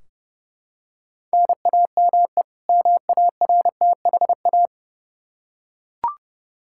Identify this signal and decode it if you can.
Morse code, heard as a keyed tone, about 32 wpm, 700 Hz. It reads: NAME MARTHA
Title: Practice QSO - Exchange 1 - Beginning - 30wpm